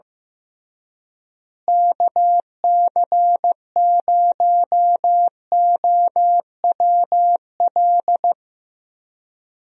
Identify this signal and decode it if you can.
Morse code, heard as a keyed tone, about 15 wpm, 700 Hz. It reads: KC0OWL
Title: USA Call Signs - 6 Characters - Repeated 3x - 15wpm